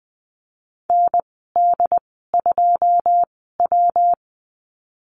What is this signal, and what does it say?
Morse code, heard as a keyed tone, about 20 wpm, spaced as for 15 wpm, 700 Hz. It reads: ND2W
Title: 161 Lesson 26g - Letter g - Call Signs 20wpm